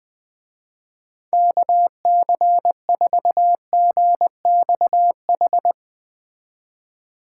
Morse code, heard as a keyed tone, about 20 wpm, 700 Hz. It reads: KC4GXH
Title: Speed Racing - USA 2x3 Call Signs 17wpm